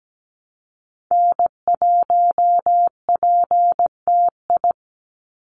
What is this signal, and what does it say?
Morse code, heard as a keyed tone, about 17 wpm, 700 Hz. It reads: N1PTI
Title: USA Call Signs 17wpm